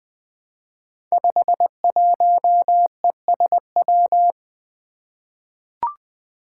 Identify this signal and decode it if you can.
Morse code, heard as a keyed tone, about 20 wpm, 700 Hz. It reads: 51ESW
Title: Five Challenging Letter-Number Combinations 20wpm